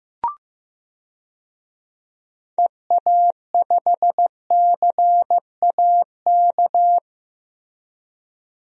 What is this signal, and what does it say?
Morse code, heard as a keyed tone, about 16 wpm, 700 Hz. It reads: EA5CAK
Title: DXCC-100 Call Signs - 6 Characters - Repeated 3x - 15wpm